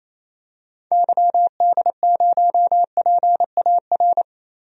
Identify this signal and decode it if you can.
Morse code, heard as a keyed tone, about 27 wpm, 700 Hz. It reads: YD0PAR
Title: DXCC-100 Call Signs - 6 Characters - Repeated 2x - 28wpm